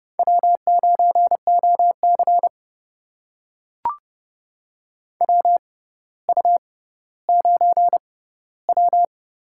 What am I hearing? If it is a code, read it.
Morse code, heard as a keyed tone, about 30 wpm, spaced as for 15 wpm, 700 Hz. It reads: W9OC WU9W
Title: USA Call Signs - 4 Characters - 6x Character Spacing - 30wpm